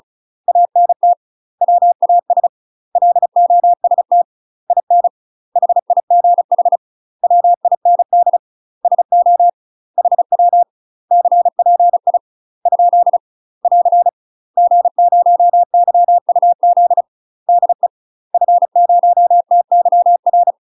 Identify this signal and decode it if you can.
Morse code, heard as a keyed tone, about 34 wpm, 700 Hz. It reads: ANT WAS LOST IN HIGH WIND SO HW CPI ? <AR> G0YUZ DE F0TYR
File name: UK MASTER QSOs TRAINING / 35WPM / UK MASTER QSOs 0004